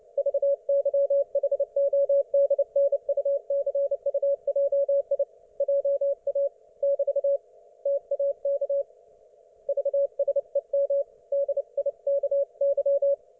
Morse code, heard as a keyed tone, about 29 wpm, 560 Hz. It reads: VYHODNUCUJI JA <BT> TAK VSEM DIKY